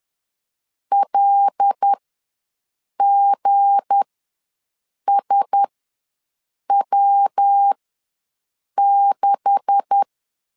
Morse code, heard as a keyed tone, about 10 wpm, spaced as for 6 wpm, 800 Hz. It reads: LGSW6
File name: LEZIONI / 9 2867 ws 10wpm 4215